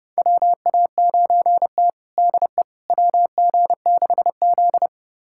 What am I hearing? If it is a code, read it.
Morse code, heard as a keyed tone, about 30 wpm, 700 Hz. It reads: WA9T DE WG6Z
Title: USA Callsign DE Callsign - 4 Characters - 30wpm